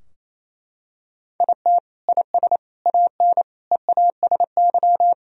Morse code, heard as a keyed tone, about 28 wpm, 700 Hz. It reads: IT IS AN EASY
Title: Sentences from Top 1000 Words - Review Part 2 - 28wpm